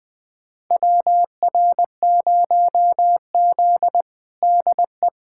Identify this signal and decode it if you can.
Morse code, heard as a keyed tone, about 20 wpm, 700 Hz. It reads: WR0Z DE
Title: USA Callsign DE Callsign - 4 Characters - 20wpm